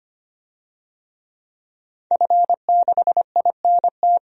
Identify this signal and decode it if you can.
Morse code, heard as a keyed tone, about 25 wpm, 700 Hz. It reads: F6INT
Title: DXCC-100 Call Signs - 5 Characters - 25wpm